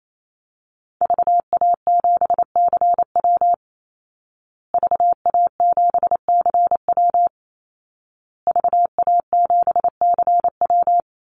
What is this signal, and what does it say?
Morse code, heard as a keyed tone, about 28 wpm, 700 Hz. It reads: VA7CW VA7CW VA7CW
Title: Call Signs Repeated 2x - 28wpm